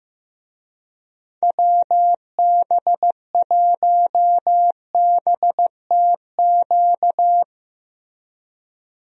Morse code, heard as a keyed tone, about 14 wpm, 700 Hz. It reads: WB1BTQ
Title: USA Call Signs - 6 Characters - Repeated 3x - 15wpm